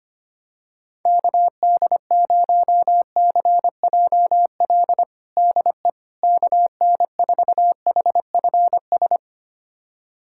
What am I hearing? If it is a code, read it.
Morse code, heard as a keyed tone, about 25 wpm, 700 Hz. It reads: KD0CJL DE KN4HFS
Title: USA Callsign DE Callsign 25wpm